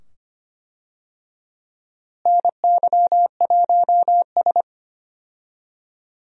Morse code, heard as a keyed tone, about 25 wpm, 700 Hz. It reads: NY1S
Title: USA Call Signs - 4 Characters - Repeated 2x - 25wpm